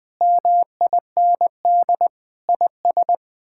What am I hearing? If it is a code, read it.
Morse code, heard as a keyed tone, about 19 wpm, 700 Hz. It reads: MIND IS
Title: Sets of 8 Words - 20wpm